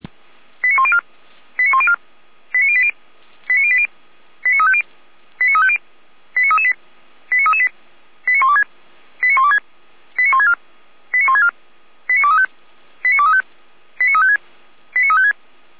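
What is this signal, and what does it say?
Five-tone selective call (ZVEI), heard as quick runs of five tones, tones of about 70 ms. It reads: 89104 89104 89080 89080 89350 89350 89338 89338 89126 89126 89164 89164 89235 89235 89356 89356
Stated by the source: ZVEI five-tone selective call received on an SDR.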